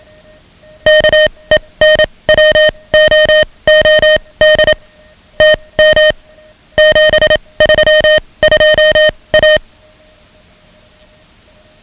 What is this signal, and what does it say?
Morse code, heard as a keyed tone, about 27 wpm, spaced as for 21 wpm, 620 Hz. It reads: KENWOOD TM 732A
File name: ham radio / CodeQuick / Code Quick 10